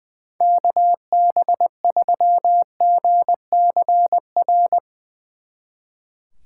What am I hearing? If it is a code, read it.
Morse code, heard as a keyed tone, about 20 wpm, 700 Hz. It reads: KB3GCR
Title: Speed Racing - USA 2x3 Call Signs 20wpm